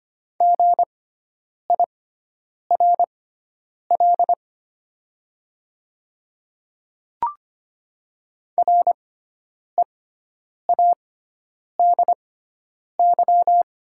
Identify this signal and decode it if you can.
Morse code, heard as a keyed tone, about 25 wpm, spaced as for 9 wpm, 700 Hz. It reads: GIRL READY
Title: 6x Character Spacing - Top 500 Words - 25wpm